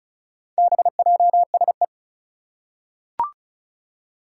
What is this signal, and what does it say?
Morse code, heard as a keyed tone, about 36 wpm, 700 Hz. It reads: BJSE
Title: Four Challenging Letter-Number Combinations 35wpm